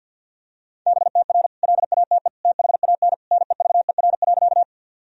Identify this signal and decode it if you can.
Morse code, heard as a keyed tone, about 49 wpm, 700 Hz. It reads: BTR LATE THAN NEVER.
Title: Sentences from Top 400 Words and Abbreviations 50wpm